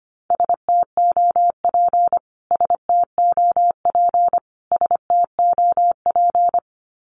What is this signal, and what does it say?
Morse code, heard as a keyed tone, about 25 wpm, 700 Hz. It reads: STOP STOP STOP
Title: Top 500 Words Repeated 25wpm